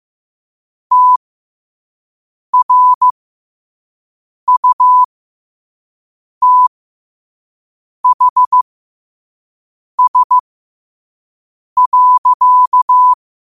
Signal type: Morse code